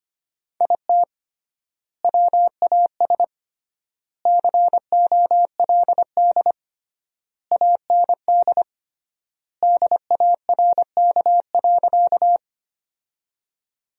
Morse code, heard as a keyed tone, about 25 wpm, 700 Hz. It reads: IT WAS COLD AND DARK.
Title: 3x Word-Spacing - Sentences from Top 500 Words - 25wpm